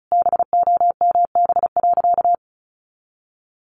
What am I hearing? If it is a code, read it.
Morse code, heard as a keyed tone, about 34 wpm, 700 Hz. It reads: BOMB.
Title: A Sound Of Thunder 35wpm